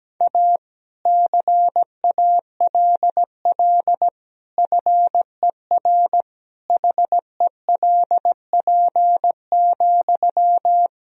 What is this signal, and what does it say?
Morse code, heard as a keyed tone, about 17 wpm, 700 Hz. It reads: A CALL FER HELP,
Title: Sentences from Top 300 Words and Abbreviations 17wpm